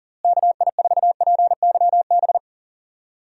Morse code, heard as a keyed tone, about 37 wpm, 700 Hz. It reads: KI4PYB
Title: USA Call Signs - 6 Characters - Repeated 3x - 40wpm